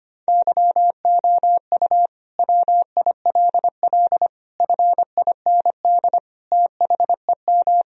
Morse code, heard as a keyed tone, about 25 wpm, 700 Hz. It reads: YOU WILL FIND THEM
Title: Sentences from Top 1000 Words - Review Part 4 - 25wpm